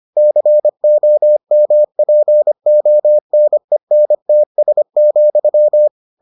Morse code, heard as a keyed tone, about 25 wpm, 600 Hz. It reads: COMPONENTS,